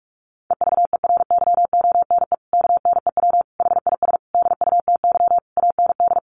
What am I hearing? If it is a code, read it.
Morse code, heard as a keyed tone, about 44 wpm, 700 Hz. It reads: EVERYONE KNEW HIS DUTY AND
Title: Sentences from Top 1500 Words 45wpm